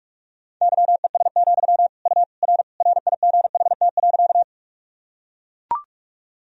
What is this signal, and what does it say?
Morse code, heard as a keyed tone, about 45 wpm, 690 Hz. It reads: YES, U R RIGHT.
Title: Sentences from Top 500 Words and Abbreviations 45wpm